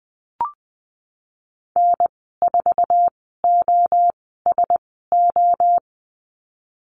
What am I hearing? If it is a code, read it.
Morse code, heard as a keyed tone, about 20 wpm, spaced as for 15 wpm, 700 Hz. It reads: N4OSO
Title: 048 Lesson 9g - Number 4 - Call Signs 20wpm